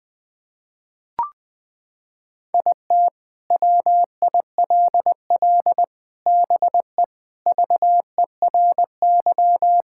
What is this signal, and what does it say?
Morse code, heard as a keyed tone, about 20 wpm, 700 Hz. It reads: IT WILL BE VERY
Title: Sentences with 7-Word Phrases - 20wpm